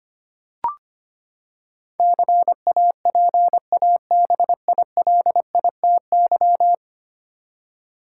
Morse code, heard as a keyed tone, about 25 wpm, 700 Hz. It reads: CAPABILITY
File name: Ten Letter Words - 25wpm